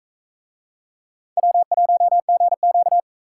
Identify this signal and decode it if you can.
Morse code, heard as a keyed tone, about 40 wpm, 700 Hz. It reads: W1GQ